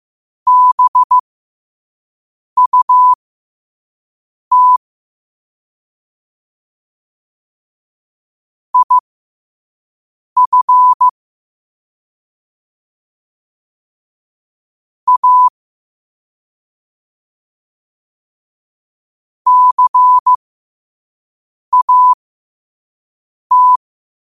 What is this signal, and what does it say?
Morse code, heard as a keyed tone, about 15 wpm, spaced as for 5 wpm, 1000 Hz. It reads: BUT IF A CAT
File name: qotdfiles / 5wpm